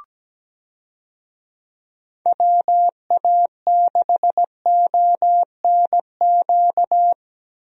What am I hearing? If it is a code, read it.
Morse code, heard as a keyed tone, about 17 wpm, 700 Hz. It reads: WA6ONQ
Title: USA Call Signs - 6 Characters - Repeated 2x - 17wpm